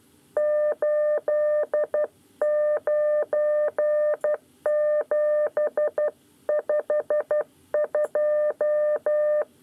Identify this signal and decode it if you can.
Morse code, heard as a keyed tone, about 12 wpm, 600 Hz. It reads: 89752
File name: 9 lezione cw 2